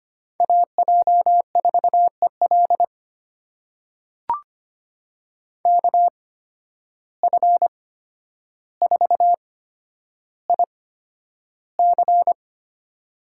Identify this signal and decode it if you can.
Morse code, heard as a keyed tone, about 25 wpm, spaced as for 11 wpm, 700 Hz. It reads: AJ4EL KF4IC
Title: USA Call Signs - 5 Characters - 8x Character Spacing - 25wpm